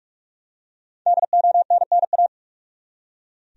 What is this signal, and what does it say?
Morse code, heard as a keyed tone, about 43 wpm, 700 Hz. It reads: DONNA
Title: USA Common Names - Max 5 Letters - 45wpm